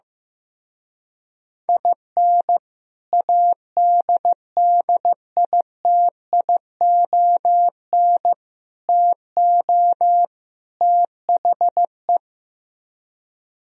Morse code, heard as a keyed tone, about 15 wpm, 700 Hz. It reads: IN ADDITION TO THE
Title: Speed-Racing - Sets of 4 Words 15wpm